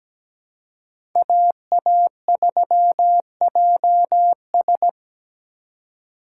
Morse code, heard as a keyed tone, about 17 wpm, 700 Hz. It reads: AA3JS
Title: USA Call Signs - 3x Character Spacing - 17wpm